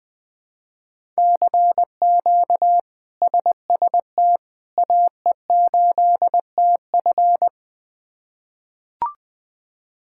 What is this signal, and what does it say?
Morse code, heard as a keyed tone, about 20 wpm, 700 Hz. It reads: CQ SST AE8TF
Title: K1USN SST QSO Practice - 20wpm